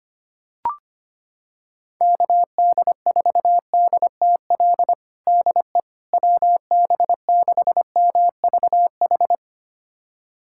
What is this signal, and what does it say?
Morse code, heard as a keyed tone, about 25 wpm, 700 Hz. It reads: KD4DTL DE WB6MVH
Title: USA Callsign DE Callsign - 6 Characters - 25wpm